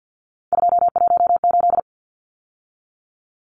Morse code, heard as a keyed tone, about 50 wpm, 700 Hz. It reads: 218
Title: Numbers - 3-Digit 50wpm